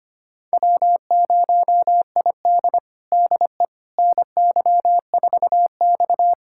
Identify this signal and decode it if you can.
Morse code, heard as a keyed tone, about 25 wpm, 700 Hz. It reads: W0ID DE NY4X
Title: USA Callsign DE Callsign - 4 Characters - 25wpm